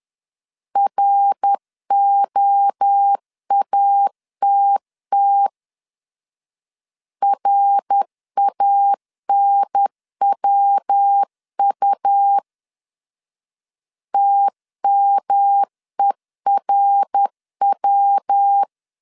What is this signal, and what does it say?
Morse code, heard as a keyed tone, about 10 wpm, 800 Hz. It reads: ROATT RANWU TMERW
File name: CW / DET / 1 3 1500C S C 10wpm 3335